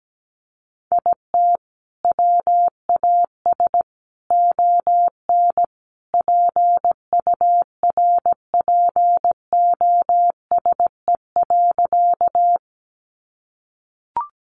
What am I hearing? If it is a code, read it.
Morse code, heard as a keyed tone, about 17 wpm, 700 Hz. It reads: IT WAS ON PURPOSE.
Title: Sentences from Top 800 Words 17wpm